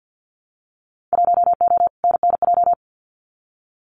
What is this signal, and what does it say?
Morse code, heard as a keyed tone, about 48 wpm, 700 Hz. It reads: 1O NNJ
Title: Winter Field Day Exchanges 50wpm